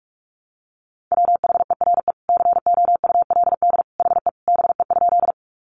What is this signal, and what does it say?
Morse code, heard as a keyed tone, about 44 wpm, 700 Hz. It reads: WHERE COULD HE BE?
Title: Sentences from Top 200 Words 45wpm